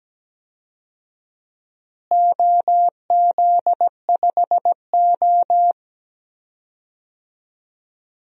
Morse code, heard as a keyed tone, about 17 wpm, 700 Hz. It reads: OZ5O
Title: DXCC-100 Call Signs - 4 Characters - Repeated 2x - 17wpm